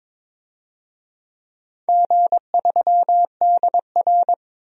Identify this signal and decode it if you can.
Morse code, heard as a keyed tone, about 22 wpm, 700 Hz. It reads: G3DR
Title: DXCC-100 Call Signs - 4 Characters - Repeated 3x - 22wpm